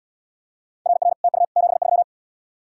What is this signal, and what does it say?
Morse code, heard as a keyed tone, about 76 wpm, 700 Hz. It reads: RR TU 73